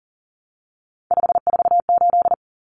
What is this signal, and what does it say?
Morse code, heard as a keyed tone, about 40 wpm, 700 Hz. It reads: 548
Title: Numbers - 3-Digit 40wpm